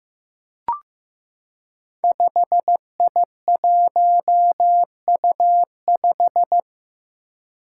Morse code, heard as a keyed tone, about 15 wpm, 700 Hz. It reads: 5I1U5
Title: Five Challenging Letter-Number Combinations 15wpm